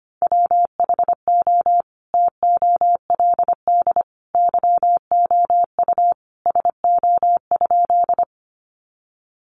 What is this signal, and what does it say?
Morse code, heard as a keyed tone, about 25 wpm, 700 Hz. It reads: WHO TOLD YOU SO?
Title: Sentences from Top 400 Words 25wpm